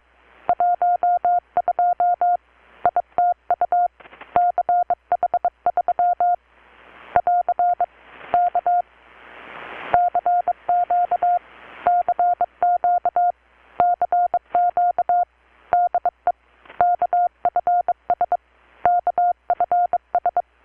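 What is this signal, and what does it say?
Morse code, heard as a keyed tone, about 25 wpm, 720 Hz. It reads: 12 ITU CH3 <AR> K CQ CQ CQ DE KFS KFS
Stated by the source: Morse from KPH's Night of Nights XX maritime radiotelegraphy event, received in Chiapas, Mexico about 3,800 km away on a Kenwood R-600 with a grounded 10 m long-wire antenna.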